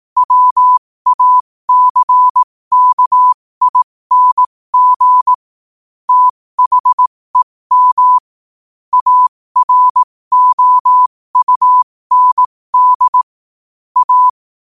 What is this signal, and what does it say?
Morse code, heard as a keyed tone, about 18 wpm, spaced as for 15 wpm, 1000 Hz. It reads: WACKING THEM AROUND A
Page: 15 wpm morse code quotes